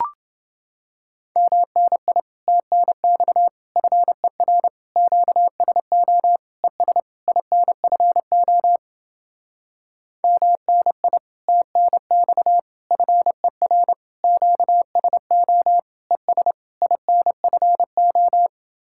Morse code, heard as a keyed tone, about 30 wpm, 700 Hz. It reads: MNI TNX FER QSO ES INFO MNI TNX FER QSO ES INFO